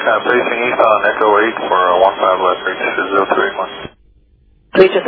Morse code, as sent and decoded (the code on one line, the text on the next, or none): none